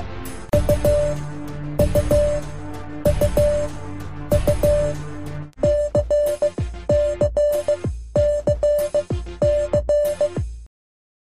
Morse code, heard as a keyed tone, about 15 wpm, spaced as for 12 wpm, 590 Hz. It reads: U U U U CCCC